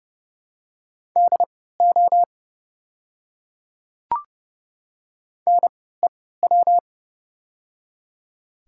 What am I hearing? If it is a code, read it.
Morse code, heard as a keyed tone, about 30 wpm, spaced as for 17 wpm, 700 Hz. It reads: DO NEW
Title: 3x Character Spacing - Top 300 Words - 30wpm